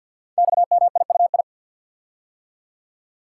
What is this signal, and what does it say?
Morse code, heard as a keyed tone, about 50 wpm, 700 Hz. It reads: XMIUI
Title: Code Groups - 5 Letters - 50wpm